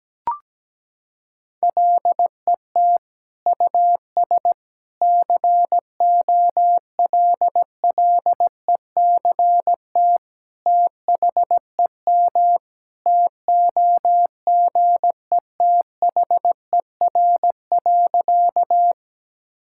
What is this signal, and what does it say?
Morse code, heard as a keyed tone, about 17 wpm, 700 Hz. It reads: LET US COLLECT THEM TOGETHER.
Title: Sentences from Top 1600 Words 17wpm